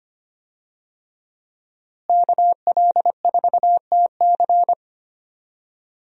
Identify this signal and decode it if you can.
Morse code, heard as a keyed tone, about 25 wpm, 700 Hz. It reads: KL4TC
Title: USA Call Signs - 5 Characters - Repeated 3x - 25wpm